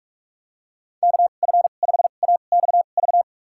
Morse code, heard as a keyed tone, about 46 wpm, spaced as for 33 wpm, 700 Hz. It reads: KF5AXV
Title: USA Call Signs - 6 Characters - 2x Character Spacing - 45wpm